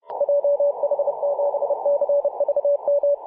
Morse code, heard as a keyed tone, about 33 wpm, 570 Hz. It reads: JH7CVM
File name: posts / mar-25-2226z-c21eu-21-020mhz-cw